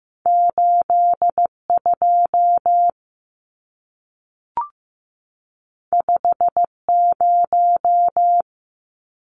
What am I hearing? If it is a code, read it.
Morse code, heard as a keyed tone, about 15 wpm, 700 Hz. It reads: 82 50
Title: Numbers - 2-Digit 15wpm